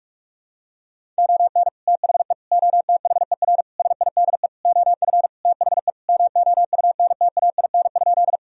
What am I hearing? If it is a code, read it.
Morse code, heard as a keyed tone, about 47 wpm, 700 Hz. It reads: ON THE OTHER SIDE OF THE MOUNTAIN?
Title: Sentences from Top 1000 Words - Review Part 1 - 45wpm